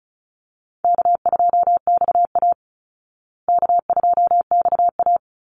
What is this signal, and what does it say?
Morse code, heard as a keyed tone, about 34 wpm, 700 Hz. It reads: K2XA K2XA
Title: Easy Call Signs Repeated 3x - 35wpm